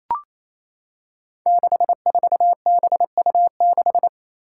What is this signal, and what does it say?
Morse code, heard as a keyed tone, about 27 wpm, 700 Hz. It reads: 64BU6
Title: Five Challenging Letter-Number Combinations 28wpm